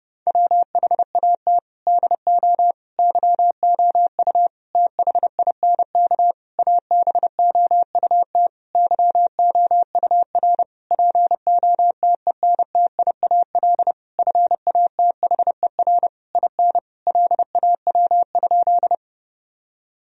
Morse code, heard as a keyed tone, about 30 wpm, 700 Hz. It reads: WHAT DO YOU THINK ABOUT YOUR POTENTIAL FATHER IN LAW?
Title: Sentences from Top 1000 Words - Review Part 4 - 30wpm